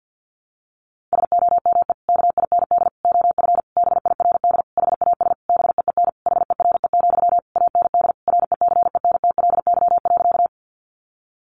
Morse code, heard as a keyed tone, about 49 wpm, 700 Hz. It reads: SOME KIND OF BIRD HAS BEEN HERE, AND RECENTLY.
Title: Sentences from Top 1200 Words 50wpm